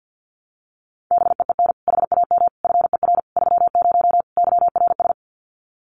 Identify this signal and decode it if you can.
Morse code, heard as a keyed tone, about 50 wpm, 700 Hz. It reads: BEEN HAM FER 30 YRS
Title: Common QSO Phrases Encore 50wpm